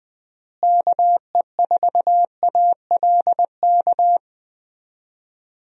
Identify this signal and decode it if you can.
Morse code, heard as a keyed tone, about 20 wpm, 700 Hz. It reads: KE4ALK
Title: USA Call Signs - 6 Characters - Repeated 3x - 20wpm